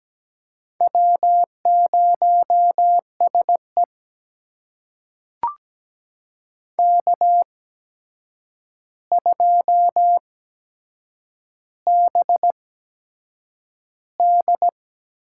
Morse code, heard as a keyed tone, about 17 wpm, spaced as for 7 wpm, 700 Hz. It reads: W0SE K2BD
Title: USA Call Signs - 4 Characters - 8x Character Spacing - 17wpm